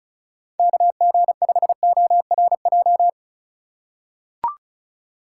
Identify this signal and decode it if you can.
Morse code, heard as a keyed tone, about 36 wpm, 700 Hz. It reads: KG5ORJ
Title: USA Call Signs - 6 Characters - 2x Character Spacing - 35wpm